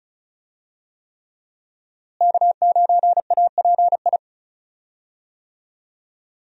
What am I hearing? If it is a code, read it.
Morse code, heard as a keyed tone, about 34 wpm, 700 Hz. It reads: K9API
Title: USA Call Signs - 5 Characters - Repeated 3x - 35wpm